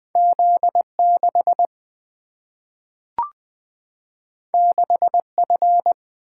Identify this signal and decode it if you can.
Morse code, heard as a keyed tone, about 20 wpm, 700 Hz. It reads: Z 6 6 F
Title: Single-Letter Single-Number 20wpm